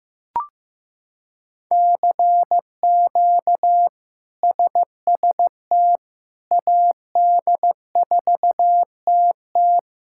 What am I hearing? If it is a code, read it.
Morse code, heard as a keyed tone, about 15 wpm, 700 Hz. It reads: CQ SST AD4TT
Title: K1USN SST QSO Practice - 15wpm